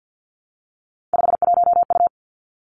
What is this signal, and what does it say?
Morse code, heard as a keyed tone, about 50 wpm, 700 Hz. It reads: 51U
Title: Three Challenging Letter-Number Combinations 50wpm